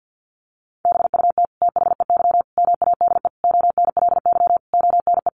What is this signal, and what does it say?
Morse code, heard as a keyed tone, about 50 wpm, 700 Hz. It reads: BUT THEY MADE ONLY ONE
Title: Sentences from Top 1300 Words 50wpm